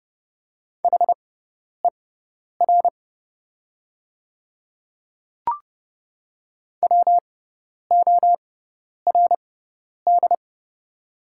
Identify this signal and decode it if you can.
Morse code, heard as a keyed tone, about 31 wpm, spaced as for 10 wpm, 700 Hz. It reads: HER WORD
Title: 6x Character Spacing - Top 100 Words - 30wpm